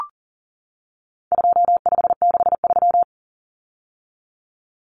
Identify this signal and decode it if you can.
Morse code, heard as a keyed tone, about 40 wpm, 700 Hz. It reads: 2563
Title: Common Cut Numbers - 4-Digit 40wpm